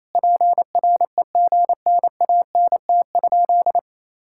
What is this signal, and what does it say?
Morse code, heard as a keyed tone, about 28 wpm, 700 Hz. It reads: PREGNANT?
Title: Sentences from Top 1000 Words - Review Part 6 - 28wpm